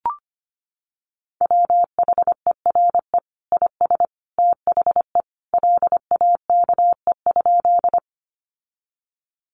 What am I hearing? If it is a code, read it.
Morse code, heard as a keyed tone, about 25 wpm, 700 Hz. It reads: WHERE IS THE LAKE?
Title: Sentences from Top 1100 Words 25wpm